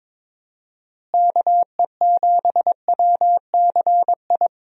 Morse code, heard as a keyed tone, about 21 wpm, 700 Hz. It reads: KE7WCI
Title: USA Call Signs - 6 Characters - Repeated 3x - 22wpm